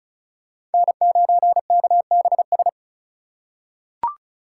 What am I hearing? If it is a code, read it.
Morse code, heard as a keyed tone, about 36 wpm, 700 Hz. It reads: N9KBS
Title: USA Call Signs - 5 Characters - 2x Character Spacing - 35wpm